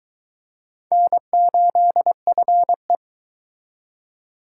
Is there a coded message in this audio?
Yes: Morse code from 0.9 to 3.0 s, heard as a keyed tone, about 23 wpm, 700 Hz. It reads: N8FE